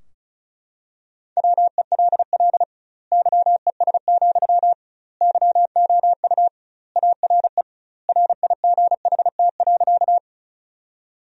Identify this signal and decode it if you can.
Morse code, heard as a keyed tone, about 34 wpm, 700 Hz. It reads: WELL YES, YOU ARE RIGHT.
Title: 2x Word-Spacing - Sentences from Top 500 Words - 35wpm